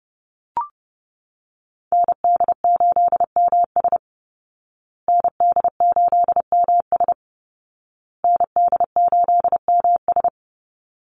Morse code, heard as a keyed tone, about 30 wpm, 700 Hz. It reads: ND8MS ND8MS ND8MS
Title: Call Signs Repeated 2x - 30wpm